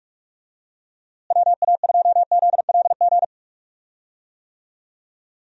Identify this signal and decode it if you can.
Morse code, heard as a keyed tone, about 46 wpm, 700 Hz. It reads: WA2ZLG
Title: USA Call Signs - 6 Characters - Repeated 3x - 45wpm